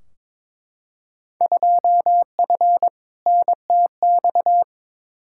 Morse code, heard as a keyed tone, about 22 wpm, 700 Hz. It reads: 2F NTX
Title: Field Day Exchanges - 22wpm